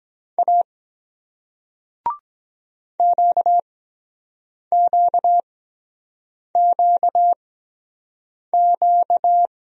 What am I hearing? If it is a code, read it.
Morse code, heard as a keyed tone, about 23 wpm, spaced as for 7 wpm, 700 Hz. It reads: A QQQQ